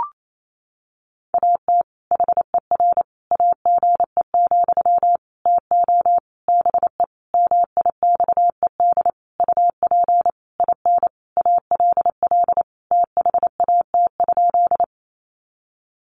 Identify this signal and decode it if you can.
Morse code, heard as a keyed tone, about 28 wpm, 700 Hz. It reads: AT HER AGE, TO BE MIXED UP IN ALL THAT?
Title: Sentences from Top 1500 Words 28wpm